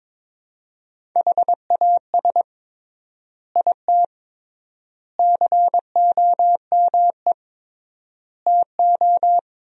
Morse code, heard as a keyed tone, about 22 wpm, 700 Hz. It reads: HAS IT COME TO
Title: 3x Word-Spacing - Sentences from Top 100 Words - 22wpm